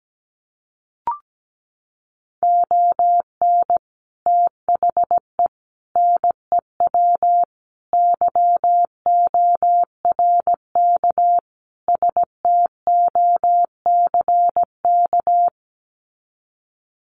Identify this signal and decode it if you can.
Morse code, heard as a keyed tone, about 17 wpm, 700 Hz. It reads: ON THE NEW YORK STOCK
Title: Sets of 5 Words Encore 17wpm